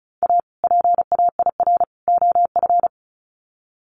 Morse code, heard as a keyed tone, about 34 wpm, 700 Hz. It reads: A PAIR OF
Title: Sets of 3 Words 35wpm